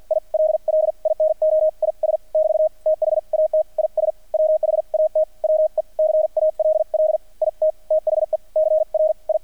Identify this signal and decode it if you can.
Morse code, heard as a keyed tone, about 48 wpm, spaced as for 41 wpm, 630 Hz. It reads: ILL INO IS <BT> THAT IS WHAT WE CALL IT THE CRI
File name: archived / CRSnet